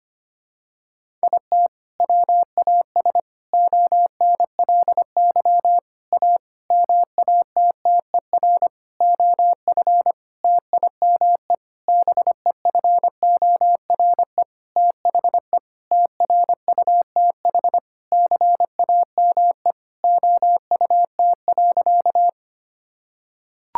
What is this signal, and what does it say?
Morse code, heard as a keyed tone, about 25 wpm, 700 Hz. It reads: IT WAS ONLY A MATTER OF TIME BEFORE THE TRUTH CAME OUT.
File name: practice / 25wpm / Sentences with 9-Word Phrases - 25wpm